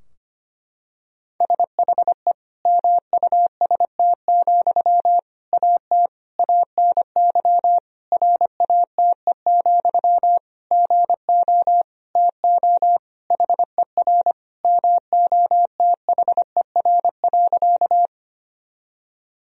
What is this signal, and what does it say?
Morse code, heard as a keyed tone, about 25 wpm, 700 Hz. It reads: SHE MUST, AT ANY RATE, GO TO HER MOTHER.
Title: Sentences from Top 1000 Words - Review Part 4 - 25wpm